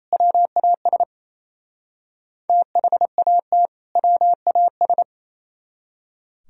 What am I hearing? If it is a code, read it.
Morse code, heard as a keyed tone, about 35 wpm, 700 Hz. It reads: WAS THAT WAS
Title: Speed-Racing - Sets of 2 Words 28wpm